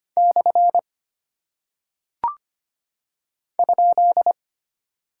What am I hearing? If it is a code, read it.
Morse code, heard as a keyed tone, about 25 wpm, 700 Hz. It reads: / ?
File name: practice / 25wpm / Single Punctuation-Prosign - 25wpm